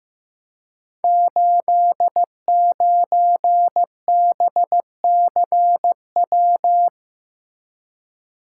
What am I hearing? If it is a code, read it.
Morse code, heard as a keyed tone, about 15 wpm, 700 Hz. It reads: 89BCW
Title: Code Groups - 5 Letter:Numbers - 15wpm